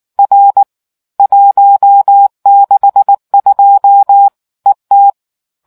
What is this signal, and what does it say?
Morse code, heard as a keyed tone, about 18 wpm, 800 Hz. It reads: R 162 ET